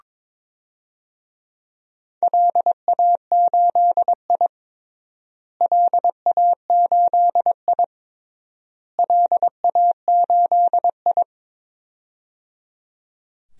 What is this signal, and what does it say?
Morse code, heard as a keyed tone, about 22 wpm, 700 Hz. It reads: LA8I LA8I LA8I
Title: DXCC-100 Call Signs - 4 Characters - Repeated 3x - 22wpm